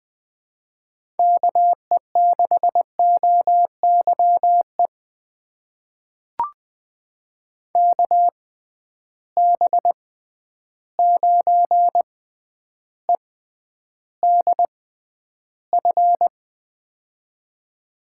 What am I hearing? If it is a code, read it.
Morse code, heard as a keyed tone, about 21 wpm, spaced as for 10 wpm, 700 Hz. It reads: KE6OYE KB9EDF
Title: USA Call Signs - 6 Characters - 6x Character Spacing - 20wpm